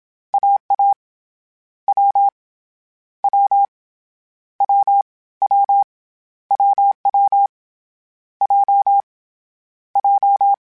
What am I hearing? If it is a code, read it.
Morse code, heard as a keyed tone, about 27 wpm, spaced as for 21 wpm, 800 Hz. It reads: AA W W WW WW J J
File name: Difficult Combos / Difficult Combos-mp3s / AWJ ShortRun@25